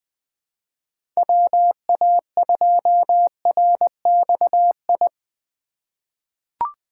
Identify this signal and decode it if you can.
Morse code, heard as a keyed tone, about 20 wpm, 700 Hz. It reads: WA2RXI
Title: USA Call Signs - 6 Characters - 4x Character Spacing - 20wpm